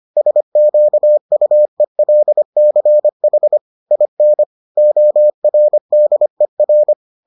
Morse code, heard as a keyed tone, about 25 wpm, 600 Hz. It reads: SQUELCH IN ORDER